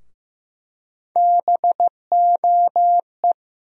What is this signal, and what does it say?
Morse code, heard as a keyed tone, about 15 wpm, 700 Hz. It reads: BOE
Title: USA Call Sign Postfixes - 3 Letters 15wpm